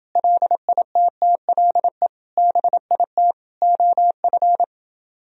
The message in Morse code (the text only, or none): LITTLE BIT OF